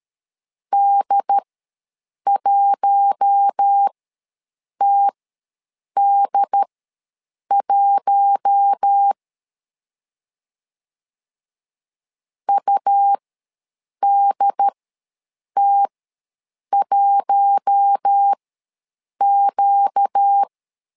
Morse code, heard as a keyed tone, about 13 wpm, spaced as for 7 wpm, 800 Hz. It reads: D1TD1 UDT1Q
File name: LEZIONI / 4 Q1 ws 12wpm 2129